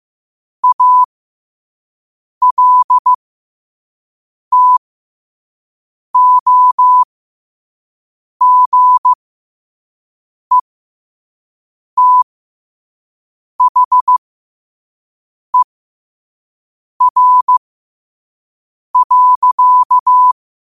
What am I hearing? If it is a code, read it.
Morse code, heard as a keyed tone, about 15 wpm, spaced as for 5 wpm, 1000 Hz. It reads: ALTOGETHER.
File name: qotdfiles / 5wpm